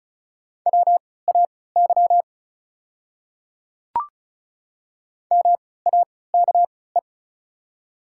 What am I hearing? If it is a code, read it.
Morse code, heard as a keyed tone, about 36 wpm, spaced as for 20 wpm, 700 Hz. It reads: WAY MAKE
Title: 3x Character Spacing - Top 200 Words - 35wpm